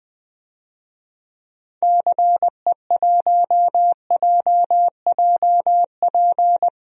Morse code, heard as a keyed tone, about 21 wpm, 700 Hz. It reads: CE1JJP
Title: DXCC-100 Call Signs - 6 Characters - Repeated 2x - 20wpm